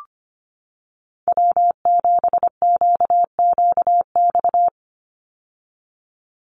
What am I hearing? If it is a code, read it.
Morse code, heard as a keyed tone, about 25 wpm, 700 Hz. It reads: W7QQX
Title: USA Call Signs 25wpm